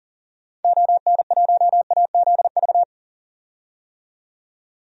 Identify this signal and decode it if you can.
Morse code, heard as a keyed tone, about 40 wpm, 700 Hz. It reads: ON1AZV
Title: DXCC-100 Call Signs - 6 Characters - 40wpm